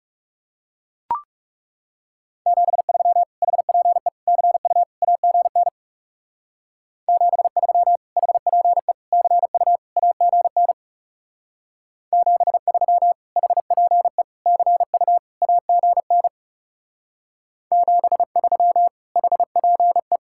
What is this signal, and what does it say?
Morse code, heard as a keyed tone, about 39 wpm, 700 Hz. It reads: 73 HPE CU AGN 73 HPE CU AGN 73 HPE CU AGN 73 HPE